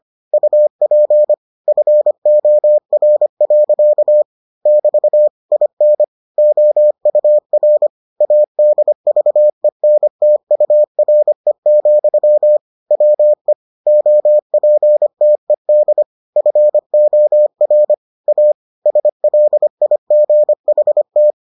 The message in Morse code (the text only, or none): UP FOR. <BT> IN OUR ADVENTURE, WE OPTED FOR A SLIGHT